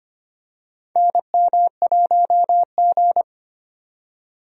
Morse code, heard as a keyed tone, about 25 wpm, 700 Hz. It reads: NM1G
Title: USA Call Signs - 4 Characters - 25wpm